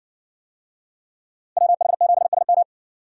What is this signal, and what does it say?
Morse code, heard as a keyed tone, about 55 wpm, 700 Hz. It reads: WH7SM